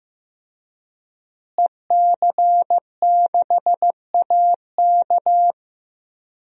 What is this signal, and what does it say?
Morse code, heard as a keyed tone, about 15 wpm, 700 Hz. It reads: EC6AK
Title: DXCC-100 Call Signs - 5 Characters - Repeated 2x - 15wpm